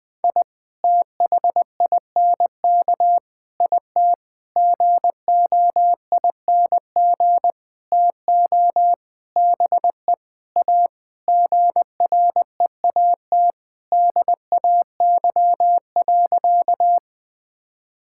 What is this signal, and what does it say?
Morse code, heard as a keyed tone, about 21 wpm, 700 Hz. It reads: I THINK IT GOING TO BE A GREAT DAY.
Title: Sentences with 6-Word Phrases - 20wpm